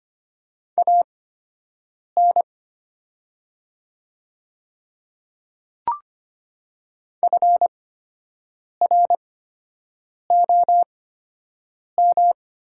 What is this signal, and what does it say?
Morse code, heard as a keyed tone, about 25 wpm, spaced as for 7 wpm, 700 Hz. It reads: AN FROM
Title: 8x Character Spacing - Top 100 Words - 25wpm